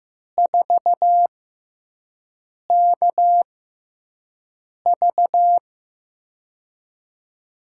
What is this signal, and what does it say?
Morse code, heard as a keyed tone, about 15 wpm, spaced as for 5 wpm, 700 Hz. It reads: 4KV
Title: USA Call Signs - 4 Characters - 6x Character Spacing - 15wpm